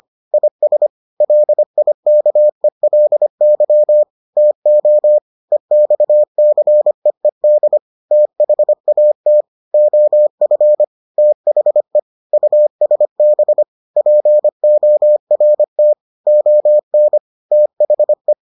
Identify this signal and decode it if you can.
Morse code, heard as a keyed tone, about 25 wpm, 600 Hz. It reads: IS LIKELY TO EXCEED THAT OF THE USB PORT ON THE